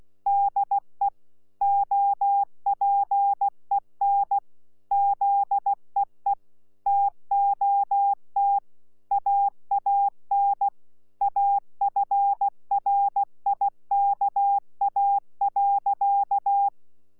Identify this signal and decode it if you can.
Morse code, heard as a keyed tone, about 16 wpm, 800 Hz. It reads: DE OPEN ZEE TOT AAN AFRIKA.
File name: Z PI4VRZ / PI4VRZ 14-jun-2025